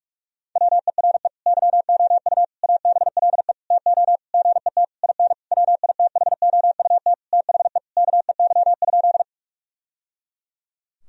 Morse code, heard as a keyed tone, about 46 wpm, 700 Hz. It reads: WERE YOU ABLE TO GET IN WITHOUT THE KEY?
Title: Sentences from Top 1000 Words - Review Part 4 - 45wpm